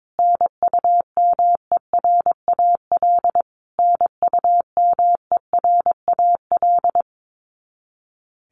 Morse code, heard as a keyed tone, about 22 wpm, 700 Hz. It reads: NUMERAL NUMERAL
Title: Top 500 Words Repeated 22wpm